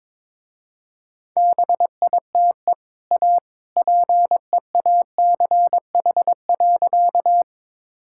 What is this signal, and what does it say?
Morse code, heard as a keyed tone, about 22 wpm, 700 Hz. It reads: BITE A PEACH.
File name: practice / 22wpm / Easy Sentences with Hints - 22wpm